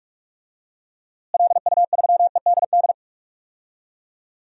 Morse code, heard as a keyed tone, about 45 wpm, 700 Hz. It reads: LU3EDD
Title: DXCC-100 Call Signs - 6 Characters - Repeated 2x - 45wpm